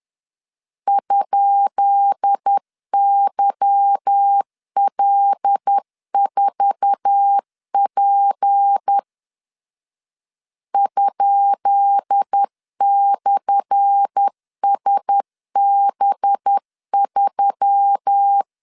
Morse code, heard as a keyed tone, about 10 wpm, 800 Hz. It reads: ?YL4P ?/SB3
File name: LEZIONI / 10 SIGNS / 10B INT 10wpm 2844